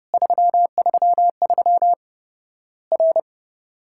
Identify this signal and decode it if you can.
Morse code, heard as a keyed tone, about 30 wpm, 700 Hz. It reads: 333 R
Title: Worked All Europe DX Contest - QTC Exchange - 30wpm